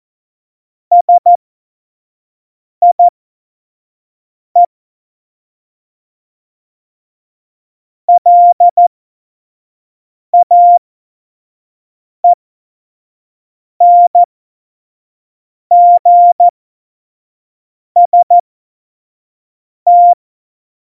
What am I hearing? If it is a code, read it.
Morse code, heard as a keyed tone, about 14 wpm, spaced as for 5 wpm, 700 Hz. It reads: SIE LAENGST
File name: Morsekurs / Witze / Witz 23. Januar 2018, 25 F70